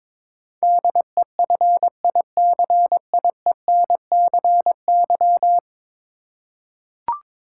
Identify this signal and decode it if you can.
Morse code, heard as a keyed tone, about 22 wpm, 700 Hz. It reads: DEFICIENCY
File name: practice / 22wpm / 10 Letter Words - First 3 Letters Given - 22wpm